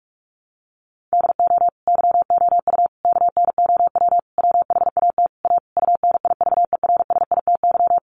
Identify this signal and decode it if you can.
Morse code, heard as a keyed tone, about 44 wpm, 700 Hz. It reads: DO YOU KNOW WHAT A UNIVERSITY
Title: Sentences from Top 1500 Words 45wpm